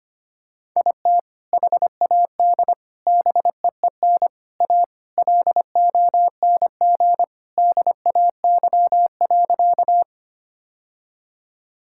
Morse code, heard as a keyed tone, about 25 wpm, 700 Hz. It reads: IT HAD BEEN A LONG DAY.
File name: practice / 25wpm / Sentences with 2-Word Phrases - 25wpm